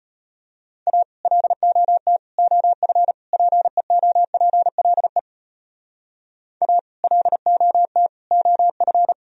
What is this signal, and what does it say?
Morse code, heard as a keyed tone, about 33 wpm, 700 Hz. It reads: A LOT OF PEOPLE A LOT OF